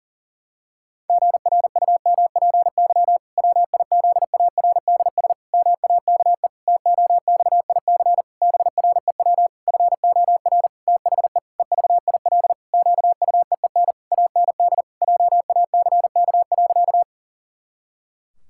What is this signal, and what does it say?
Morse code, heard as a keyed tone, about 40 wpm, 700 Hz. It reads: GRUMPY WIZARDS MAKE TOXIC BREW FOR THE EVIL QUEEN AND JACK.
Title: Pangram Challenge v2 - 40wpm